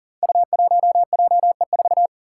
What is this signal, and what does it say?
Morse code, heard as a keyed tone, about 39 wpm, 700 Hz. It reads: U1JE4
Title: Five Challenging Letter-Number Combinations 40wpm